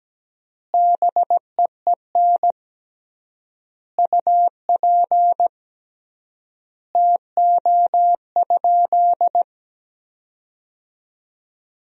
Morse code, heard as a keyed tone, about 17 wpm, 700 Hz. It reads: BEEN UP TO?
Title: 3x Word-Spacing - Sentences from Top 100 Words - 17wpm